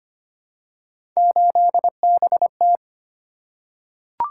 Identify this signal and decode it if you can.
Morse code, heard as a keyed tone, about 25 wpm, 700 Hz. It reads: 8BT
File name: Three Letter-Number Combinations - 25wpm